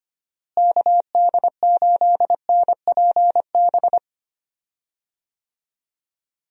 Morse code, heard as a keyed tone, about 25 wpm, 700 Hz. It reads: KD8NPB
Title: USA Call Signs - 6 Characters - Repeated 3x - 25wpm